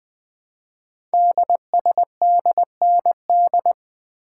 Morse code, heard as a keyed tone, about 20 wpm, 700 Hz. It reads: DSDND
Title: Code Groups - 5 Letters - 20wpm